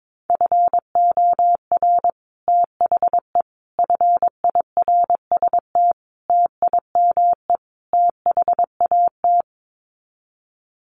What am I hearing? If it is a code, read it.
Morse code, heard as a keyed tone, about 22 wpm, 700 Hz. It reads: FOR THE FIRST TIME THAT
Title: Sets of 5 Words Encore 22wpm